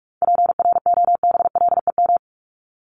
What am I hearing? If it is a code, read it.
Morse code, heard as a keyed tone, about 44 wpm, 700 Hz. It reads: PROBLEM
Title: Top 500 Words Repeated 45wpm